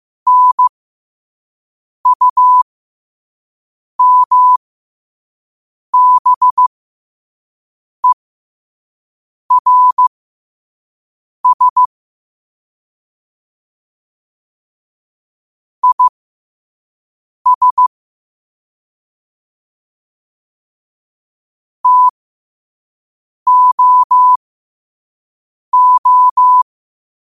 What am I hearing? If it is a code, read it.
Morse code, heard as a keyed tone, about 15 wpm, spaced as for 5 wpm, 1000 Hz. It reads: NUMBERS IS TOO